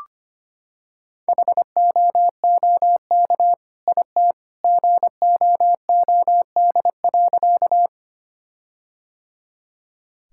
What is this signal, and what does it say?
Morse code, heard as a keyed tone, about 25 wpm, 700 Hz. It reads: HOOK IT GOOD.
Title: Sentences from Top 1600 Words 25wpm